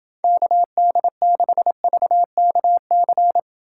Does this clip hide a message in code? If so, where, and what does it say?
Yes: Morse code from 0.2 to 3.4 s, heard as a keyed tone, about 28 wpm, 700 Hz. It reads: KD6VKC